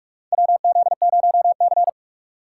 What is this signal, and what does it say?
Morse code, heard as a keyed tone, about 44 wpm, 700 Hz. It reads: WZ0C
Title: USA Call Signs - 2x Character Spacing - 45wpm